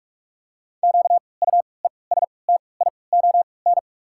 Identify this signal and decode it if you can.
Morse code, heard as a keyed tone, about 46 wpm, spaced as for 26 wpm, 700 Hz. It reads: QUESTION
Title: 3x Character Spacing - Top 500 Words - 45wpm